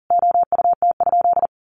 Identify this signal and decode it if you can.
Morse code, heard as a keyed tone, about 40 wpm, 700 Hz. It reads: OUT?
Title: Sentences from Top 100 Words 40wpm